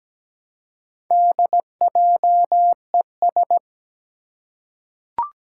Morse code, heard as a keyed tone, about 17 wpm, 700 Hz. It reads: DJES
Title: Four Challenging Letter-Number Combinations 17wpm